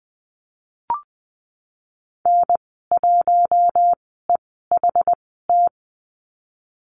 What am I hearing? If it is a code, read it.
Morse code, heard as a keyed tone, about 19 wpm, spaced as for 14 wpm, 700 Hz. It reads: N1EHT
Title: 088 Lesson 15g - Number 5 - Call Signs 20wpm